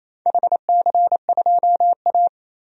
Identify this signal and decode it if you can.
Morse code, heard as a keyed tone, about 28 wpm, 700 Hz. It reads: HC2A
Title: DXCC-100 Call Signs - 4 Characters - 28wpm